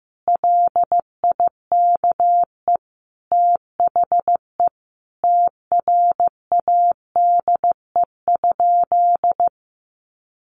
Morse code, heard as a keyed tone, about 15 wpm, 700 Hz. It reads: LIKE THE TRADE?
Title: Sentences from Top 600 Words 15wpm